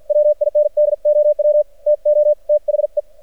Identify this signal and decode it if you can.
Morse code, heard as a keyed tone, about 53 wpm, spaced as for 46 wpm, 600 Hz. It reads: WINDOW TO THE
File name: archived / CRSnet